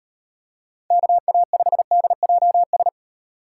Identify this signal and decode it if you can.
Morse code, heard as a keyed tone, about 33 wpm, 700 Hz. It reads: KA5DJS